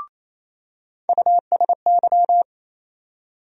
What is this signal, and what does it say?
Morse code, heard as a keyed tone, about 28 wpm, 700 Hz. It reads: USY
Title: USA Call Sign Postfixes - 3 Letters 28wpm